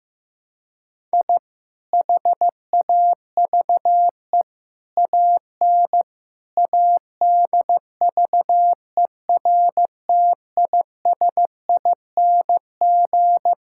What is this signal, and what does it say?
Morse code, heard as a keyed tone, about 16 wpm, 700 Hz. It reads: I HAVE AN ADVERTISING
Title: Sentences from Top 1000 Words - Review Part 6 - 15wpm